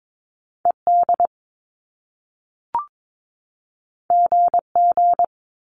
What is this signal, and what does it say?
Morse code, heard as a keyed tone, about 22 wpm, 700 Hz. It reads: ED GG
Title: Top Two Letter Combinations in Words 22wpm